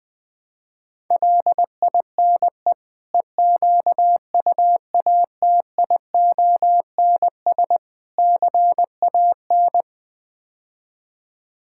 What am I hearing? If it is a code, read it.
Morse code, heard as a keyed tone, about 21 wpm, 700 Hz. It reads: LINE EQUATIONS CAN
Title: Build A Phrase - 5-Word Phrases - 20wpm